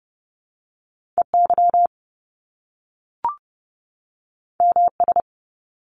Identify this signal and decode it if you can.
Morse code, heard as a keyed tone, about 30 wpm, 700 Hz. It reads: EY MS
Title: Top Two Letter Combinations in Words 30wpm